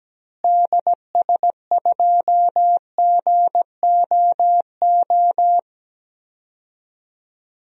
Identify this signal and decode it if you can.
Morse code, heard as a keyed tone, about 17 wpm, 700 Hz. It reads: DS2GOO
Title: DXCC-100 Call Signs - 6 Characters - Repeated 3x - 17wpm